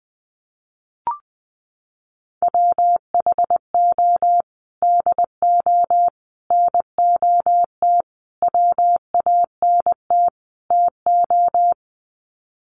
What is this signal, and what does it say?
Morse code, heard as a keyed tone, about 20 wpm, 700 Hz. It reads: WHO DO NOT WANT TO
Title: Sets of 5 Words Encore2 20wpm